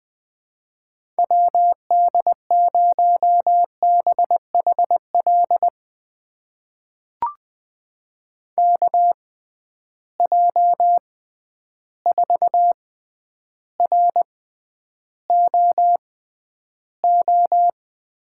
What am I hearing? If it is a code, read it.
Morse code, heard as a keyed tone, about 20 wpm, spaced as for 10 wpm, 700 Hz. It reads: WD0BHL KJ4ROO
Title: USA Call Signs - 6 Characters - 6x Character Spacing - 20wpm